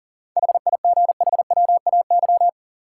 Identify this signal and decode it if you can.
Morse code, heard as a keyed tone, about 37 wpm, 700 Hz. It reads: HIGHWAY
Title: Seven Letter Words - 40wpm